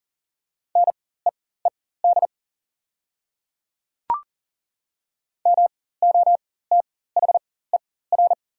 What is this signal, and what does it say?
Morse code, heard as a keyed tone, about 41 wpm, spaced as for 19 wpm, 700 Hz. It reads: NEED MOTHER
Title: 4x Character Spacing - Top 200 Words - 40wpm